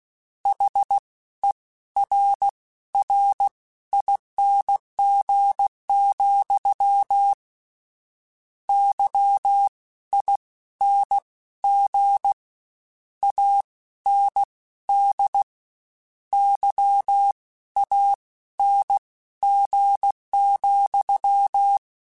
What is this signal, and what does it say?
Morse code, heard as a keyed tone, about 16 wpm, spaced as for 12 wpm, 800 Hz. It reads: HERRING, YING AND YANG,